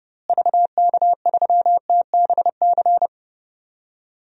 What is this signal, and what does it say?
Morse code, heard as a keyed tone, about 30 wpm, 700 Hz. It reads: VK3TBC
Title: DXCC-100 Call Signs - 6 Characters - Repeated 3x - 30wpm